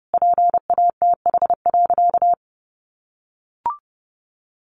Morse code, heard as a keyed tone, about 30 wpm, 700 Hz. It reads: PATH.
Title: A Sound Of Thunder 30wpm